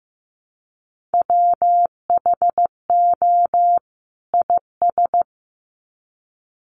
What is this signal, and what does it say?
Morse code, heard as a keyed tone, about 15 wpm, 700 Hz. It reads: WHO IS
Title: Sets of 2 Words 15wpm